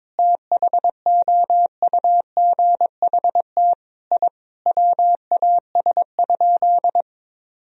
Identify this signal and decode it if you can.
Morse code, heard as a keyed tone, about 22 wpm, 700 Hz. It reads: THOUGHT I WAS?
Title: Sentences from Top 1000 Words - Review Part 3 - 22wpm